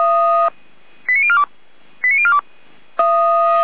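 Five-tone selective call (ZVEI), heard as quick runs of five tones, tones of about 70 ms. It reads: 89042 89042
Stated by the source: ZVEI five-tone selective call received on an SDR.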